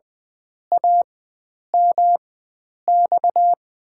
Morse code, heard as a keyed tone, about 20 wpm, spaced as for 9 wpm, 700 Hz. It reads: AMX
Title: USA Call Signs - 5 Characters - 4x Character Spacing - 20wpm